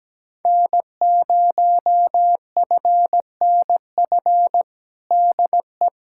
Morse code, heard as a keyed tone, about 17 wpm, 700 Hz. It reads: N0FNF DE
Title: USA Callsign DE Callsign - 5 Characters - 17wpm